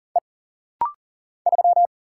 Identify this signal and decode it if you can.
Morse code, heard as a keyed tone, about 40 wpm, 700 Hz. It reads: E 3
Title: Single Letter-Number - Rapid-Fire 40wpm